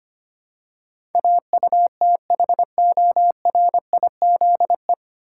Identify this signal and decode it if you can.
Morse code, heard as a keyed tone, about 25 wpm, 700 Hz. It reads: AUTHORIZE
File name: Nine Letter Words - 25wpm